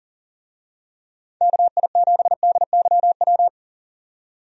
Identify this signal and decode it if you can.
Morse code, heard as a keyed tone, about 40 wpm, 700 Hz. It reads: KI7DYW
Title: USA Call Signs - 6 Characters - 40wpm